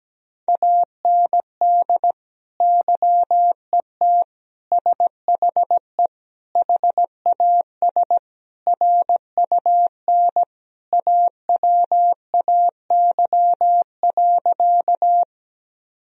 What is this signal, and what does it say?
Morse code, heard as a keyed tone, about 17 wpm, 700 Hz. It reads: AND YET SHE HAS RUN AWAY.
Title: Sentences from Top 1000 Words - Review Part 1 - 17wpm